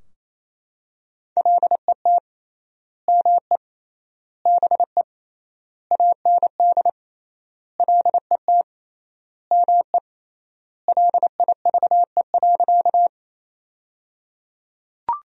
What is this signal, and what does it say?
Morse code, heard as a keyed tone, about 28 wpm, 700 Hz. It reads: LET ME BE AND LET ME LIVE.
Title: 3x Word-Spacing - Sentences from Top 300 Words - 28wpm